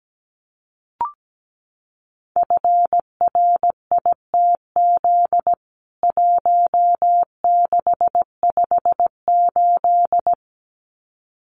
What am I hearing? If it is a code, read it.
Morse code, heard as a keyed tone, about 17 wpm, 700 Hz. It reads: FRITZ 1658
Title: CWT Exchange 17wpm